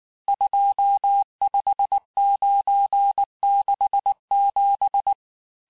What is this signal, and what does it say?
Morse code, heard as a keyed tone, about 19 wpm, 800 Hz. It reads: 25967
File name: cours-du-lundi files / cours-1583--texte-chiffre-18-mpm